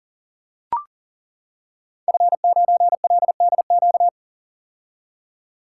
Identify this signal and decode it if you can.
Morse code, heard as a keyed tone, about 42 wpm, 700 Hz. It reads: F9LDQ
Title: Code Groups - 5 Letter:Numbers - 40wpm